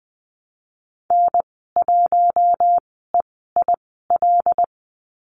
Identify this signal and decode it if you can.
Morse code, heard as a keyed tone, about 20 wpm, spaced as for 15 wpm, 700 Hz. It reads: N1EIL
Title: 082 Lesson 14g - Number 2 - Call Signs 20wpm